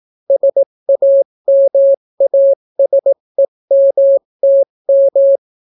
Morse code, heard as a keyed tone, about 18 wpm, 550 Hz. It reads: SAMASEMTM